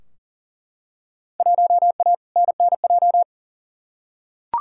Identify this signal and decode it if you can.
Morse code, heard as a keyed tone, about 40 wpm, 700 Hz. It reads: 1A NNJ
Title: Field Day Exchanges Encore - 40wpm